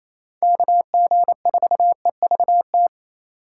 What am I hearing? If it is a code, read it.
Morse code, heard as a keyed tone, about 28 wpm, 700 Hz. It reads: KG4EVT
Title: USA Call Signs - 6 Characters - 3x Character Spacing - 28wpm